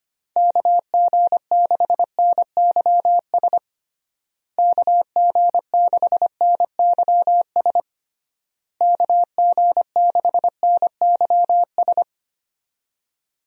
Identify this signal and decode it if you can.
Morse code, heard as a keyed tone, about 25 wpm, 700 Hz. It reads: KG6NYS KG6NYS KG6NYS
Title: USA Call Signs - 6 Characters - Repeated 3x - 25wpm